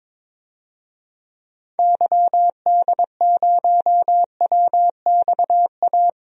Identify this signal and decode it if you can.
Morse code, heard as a keyed tone, about 22 wpm, 700 Hz. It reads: YD0WXA
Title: DXCC-100 Call Signs - 6 Characters - 22wpm